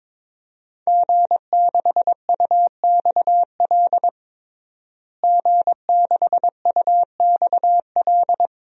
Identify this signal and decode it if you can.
Morse code, heard as a keyed tone, about 22 wpm, 700 Hz. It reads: G6UXL G6UXL
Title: DXCC-100 Call Signs - 5 Characters - Repeated 3x - 22wpm